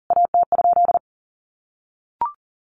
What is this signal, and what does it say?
Morse code, heard as a keyed tone, about 43 wpm, 700 Hz. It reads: AT?
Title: A Sound Of Thunder 40wpm